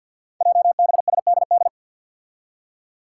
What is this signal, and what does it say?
Morse code, heard as a keyed tone, about 46 wpm, 700 Hz. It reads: JBSDD
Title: Five Challenging Letter-Number Combinations 50wpm